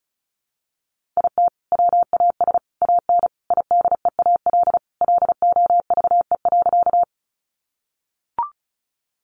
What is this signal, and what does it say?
Morse code, heard as a keyed tone, about 34 wpm, 700 Hz. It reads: IT WAS AN IDEAL LOVE.
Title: Sentences from Top 1400 Words 35wpm